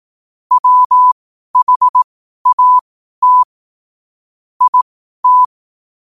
Morse code, heard as a keyed tone, about 18 wpm, spaced as for 12 wpm, 1000 Hz. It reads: WHAT IT
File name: qotdfiles / 12wpm